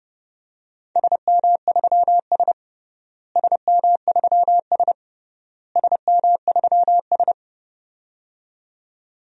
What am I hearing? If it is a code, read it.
Morse code, heard as a keyed tone, about 30 wpm, 700 Hz. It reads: SM3S SM3S SM3S
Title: DXCC-100 Call Signs - 4 Characters - Repeated 3x - 30wpm